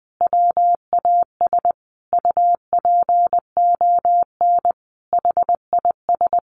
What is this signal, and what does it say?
Morse code, heard as a keyed tone, about 20 wpm, 700 Hz. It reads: WAS UPON HIS
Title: Sentences from Top 1100 Words 20wpm